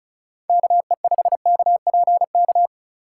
Morse code, heard as a keyed tone, about 43 wpm, 700 Hz. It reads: KE5KPK
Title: Speed Racing - USA Call Signs 35wpm